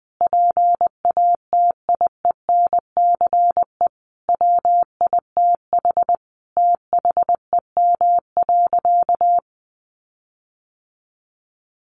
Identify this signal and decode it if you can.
Morse code, heard as a keyed tone, about 19 wpm, 700 Hz. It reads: PATIENCE WITH THEM.
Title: Sentences from Top 1300 Words 20wpm